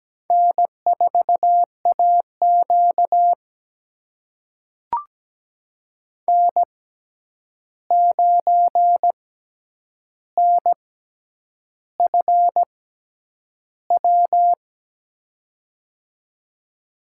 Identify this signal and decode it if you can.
Morse code, heard as a keyed tone, about 17 wpm, spaced as for 8 wpm, 700 Hz. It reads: N4AQ N9NFW
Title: USA Call Signs - 6x Character Spacing - 17wpm